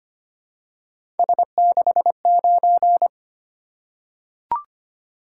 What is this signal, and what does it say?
Morse code, heard as a keyed tone, about 25 wpm, 700 Hz. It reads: S69
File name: Three Letter-Number Combinations - 25wpm